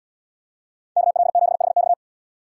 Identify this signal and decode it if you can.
Morse code, heard as a keyed tone, about 77 wpm, 700 Hz. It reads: KL7HX